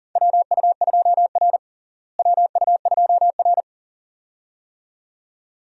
Morse code, heard as a keyed tone, about 37 wpm, 700 Hz. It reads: WU2R WU2R
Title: USA Call Signs - 4 Characters - Repeated 2x - 40wpm